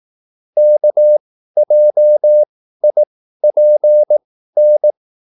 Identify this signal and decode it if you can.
Morse code, heard as a keyed tone, about 18 wpm, spaced as for 13 wpm, 600 Hz. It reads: KJIPN
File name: QSL / CW lesson